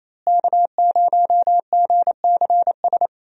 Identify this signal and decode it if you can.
Morse code, heard as a keyed tone, about 28 wpm, 700 Hz. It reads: K0GCS
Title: USA Call Signs - 5 Characters - Repeated 2x - 28wpm